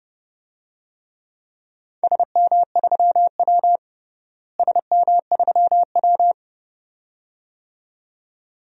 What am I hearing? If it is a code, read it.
Morse code, heard as a keyed tone, about 30 wpm, 700 Hz. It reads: SM3W SM3W
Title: DXCC-100 Call Signs - 4 Characters - Repeated 2x - 30wpm